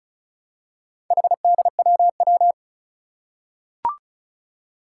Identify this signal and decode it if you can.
Morse code, heard as a keyed tone, about 36 wpm, 700 Hz. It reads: HDWW
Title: Four Challenging Letter-Number Combinations 35wpm